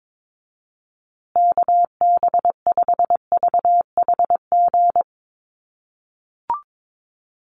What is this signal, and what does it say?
Morse code, heard as a keyed tone, about 22 wpm, 700 Hz. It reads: KB5VHG
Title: USA Call Signs Repeated 3x 22wpm